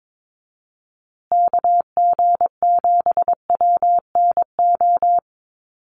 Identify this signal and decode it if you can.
Morse code, heard as a keyed tone, about 22 wpm, 700 Hz. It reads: KG7WNO
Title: USA Call Signs Repeated 2x 22wpm